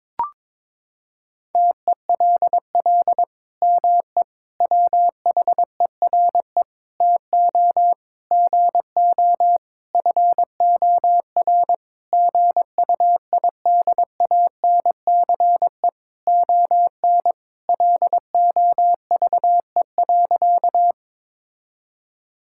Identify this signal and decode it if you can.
Morse code, heard as a keyed tone, about 22 wpm, 700 Hz. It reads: TELL ME WHERE TO GO FOR GUIDANCE ON LOVE.
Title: Sentences from Top 1600 Words 22wpm